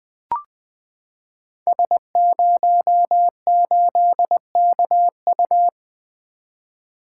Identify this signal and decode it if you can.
Morse code, heard as a keyed tone, about 20 wpm, 700 Hz. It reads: S08KU
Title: Code Groups - 5 Letter:Numbers - 20wpm